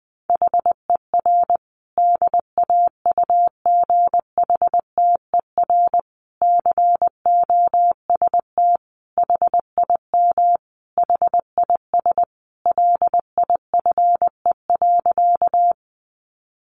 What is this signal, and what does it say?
Morse code, heard as a keyed tone, about 20 wpm, 700 Hz. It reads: HER DAUGHTER COST HIM HIS LIFE.
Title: Sentences from Top 1500 Words 20wpm